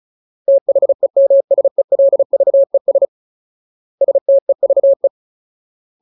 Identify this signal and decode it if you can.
Morse code, heard as a keyed tone, about 34 wpm, 550 Hz. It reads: THEMSELVES STEVE
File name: Ultimate Word List / 35WPM Ultimate